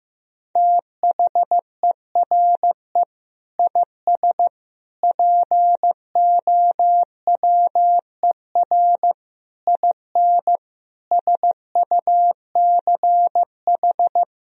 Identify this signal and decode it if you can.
Morse code, heard as a keyed tone, about 15 wpm, 700 Hz. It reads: THERE IS POWER IN SUCH
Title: Sentences from Top 1000 Words - Review Part 4 - 15wpm